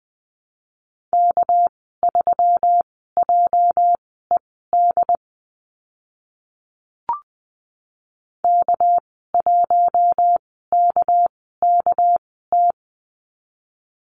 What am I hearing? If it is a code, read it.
Morse code, heard as a keyed tone, about 20 wpm, spaced as for 14 wpm, 700 Hz. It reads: K3JED K1KKT
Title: 231 Lesson 37g - Letter x - Call Signs 20wpm